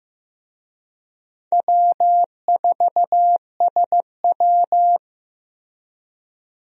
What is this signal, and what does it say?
Morse code, heard as a keyed tone, about 15 wpm, 700 Hz. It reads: W4SW
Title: USA Call Signs - 4 Characters - Repeated 3x - 15wpm